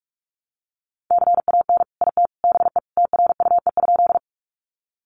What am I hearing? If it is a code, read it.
Morse code, heard as a keyed tone, about 44 wpm, 700 Hz. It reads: CAN IT BE TRUE?
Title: Sentences from Top 400 Words 45wpm